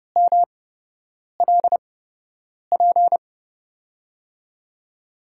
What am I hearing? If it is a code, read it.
Morse code, heard as a keyed tone, about 30 wpm, spaced as for 8 wpm, 700 Hz. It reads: MLP
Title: USA Call Signs - 5 Characters - 8x Character Spacing - 30wpm